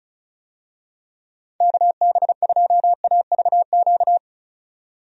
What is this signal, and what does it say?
Morse code, heard as a keyed tone, about 34 wpm, 700 Hz. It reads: KB2AVQ
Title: USA Call Signs - 6 Characters - 35wpm